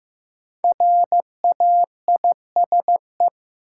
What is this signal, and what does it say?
Morse code, heard as a keyed tone, about 15 wpm, 700 Hz. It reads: RAISE
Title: Sentences from Top 1400 Words - New Word Drill Alt - 15wpm